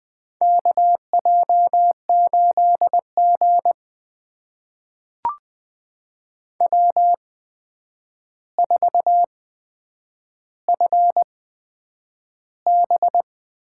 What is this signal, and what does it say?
Morse code, heard as a keyed tone, about 20 wpm, spaced as for 9 wpm, 700 Hz. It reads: KJ8G W4FB
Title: USA Call Signs - 4 Characters - 8x Character Spacing - 20wpm